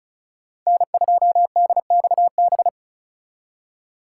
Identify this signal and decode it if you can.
Morse code, heard as a keyed tone, about 43 wpm, 700 Hz. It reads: N2DXB
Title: Speed Racing - USA 1x3 Call Signs 35wpm